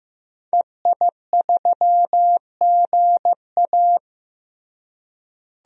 Morse code, heard as a keyed tone, about 16 wpm, 700 Hz. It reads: EI3GA
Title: DXCC-100 Call Signs - 5 Characters - Repeated 2x - 15wpm